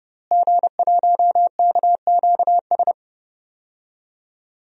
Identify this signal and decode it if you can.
Morse code, heard as a keyed tone, about 30 wpm, 700 Hz. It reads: G1KQS
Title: DXCC-100 Call Signs - 5 Characters - Repeated 2x - 30wpm